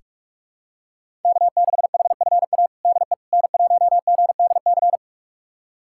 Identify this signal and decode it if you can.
Morse code, heard as a keyed tone, about 46 wpm, 700 Hz. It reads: K6HFA DE N1GDC
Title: USA Callsign DE Callsign - 5 Characters - 45wpm